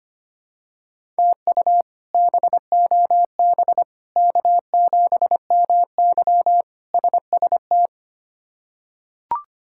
Morse code, heard as a keyed tone, about 25 wpm, 700 Hz. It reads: TU BOB K7MY SST
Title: K1USN SST QSO Practice - 25wpm